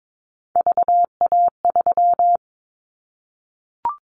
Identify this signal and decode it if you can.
Morse code, heard as a keyed tone, about 22 wpm, 700 Hz. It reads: VA3
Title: Callsign Prefixes for Top 25 DXCC - 22wpm